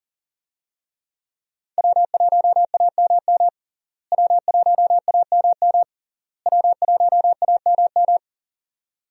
Morse code, heard as a keyed tone, about 40 wpm, 700 Hz. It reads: W1AMM W1AMM W1AMM
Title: USA Call Signs - 5 Characters - Repeated 3x - 40wpm